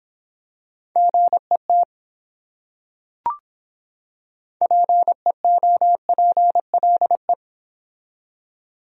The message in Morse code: GET PEOPLE

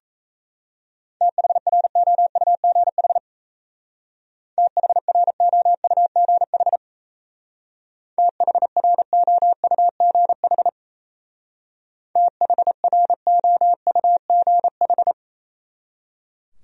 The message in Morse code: THROUGH THROUGH THROUGH THROUGH

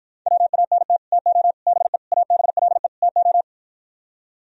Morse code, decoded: WANT TO BE ABLE TO

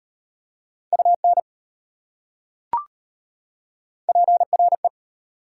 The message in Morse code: UN PRE

